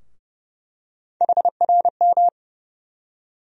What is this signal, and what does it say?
Morse code, heard as a keyed tone, about 30 wpm, 700 Hz. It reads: HRM
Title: USA Call Sign Postfixes - 3 Letters 30wpm